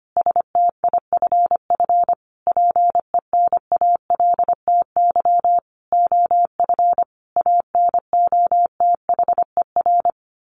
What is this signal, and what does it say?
Morse code, heard as a keyed tone, about 25 wpm, 700 Hz. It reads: STIFF PENALTY OF ANOTHER
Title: A Sound Of Thunder 25wpm